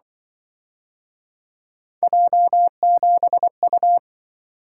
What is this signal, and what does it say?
Morse code, heard as a keyed tone, about 25 wpm, 700 Hz. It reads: J7U
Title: Speed-Racing - Three Letter-Number Combinations - 20wpm